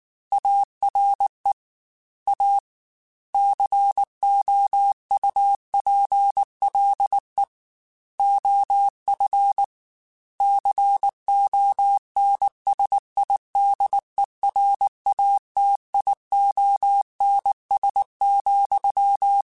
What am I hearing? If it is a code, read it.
Morse code, heard as a keyed tone, about 19 wpm, 800 Hz. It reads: ARE A COUPLE OF CONSIDERATIONS,
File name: Technical Subjects / Technical Subjects-mp3s / Sw Pwr Sup @18